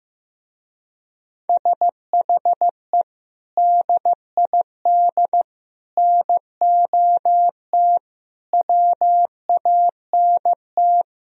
Morse code, heard as a keyed tone, about 15 wpm, 700 Hz. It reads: SHE DID NOT WANT
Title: Speed-Racing - Sets of 5 Words 15wpm